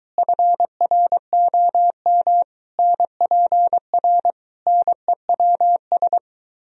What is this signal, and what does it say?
Morse code, heard as a keyed tone, about 22 wpm, 700 Hz. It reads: FROM NPR NEWS